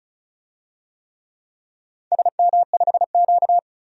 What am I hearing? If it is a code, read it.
Morse code, heard as a keyed tone, about 36 wpm, 700 Hz. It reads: SM5Q
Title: DXCC-100 Call Signs - 4 Characters - 35wpm